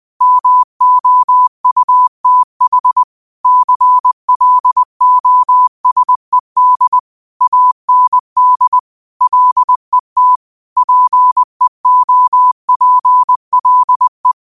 Morse code, heard as a keyed tone, about 20 wpm, 1000 Hz. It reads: MOUTH CLOSED AND LET PEOPLE
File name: qotdfiles / 20wpm